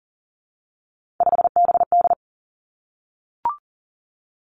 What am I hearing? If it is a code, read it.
Morse code, heard as a keyed tone, about 40 wpm, 700 Hz. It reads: 5BD
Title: Three Challenging Letter-Number Combinations 40wpm